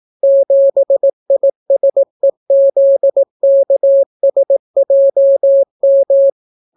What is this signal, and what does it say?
Morse code, heard as a keyed tone, about 18 wpm, 550 Hz. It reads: 7ISEZKSJM